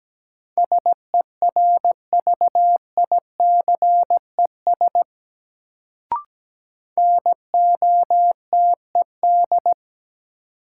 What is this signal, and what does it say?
Morse code, heard as a keyed tone, about 17 wpm, 700 Hz. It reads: SERVICES NOTED
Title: Sets of 5 Words - New Word Drill - 17wpm